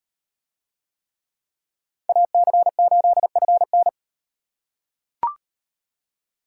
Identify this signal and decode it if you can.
Morse code, heard as a keyed tone, about 36 wpm, 700 Hz. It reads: AC8FN